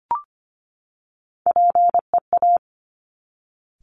Morse code, heard as a keyed tone, about 24 wpm, 700 Hz. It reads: PEA
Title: Top Three Letter Combinations in Words 25wpm